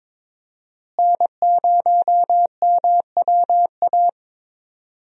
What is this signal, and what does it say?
Morse code, heard as a keyed tone, about 22 wpm, 700 Hz. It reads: N0MWA
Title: USA Call Signs - 5 Characters - Repeated 2x - 22wpm